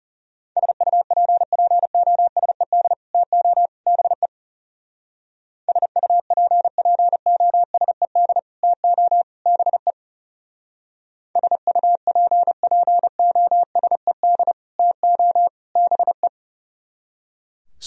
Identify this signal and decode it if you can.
Morse code, heard as a keyed tone, about 34 wpm, 700 Hz. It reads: SUPPOSED TO BE SUPPOSED TO BE SUPPOSED TO BE
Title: Speed-Racing - Sets of 3 Words 30wpm